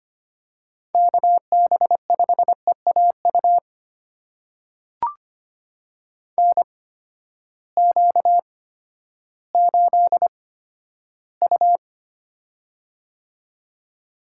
Morse code, heard as a keyed tone, about 25 wpm, spaced as for 12 wpm, 700 Hz. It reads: KB5EAU NQ8U
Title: USA Call Signs - 8x Character Spacing - 25wpm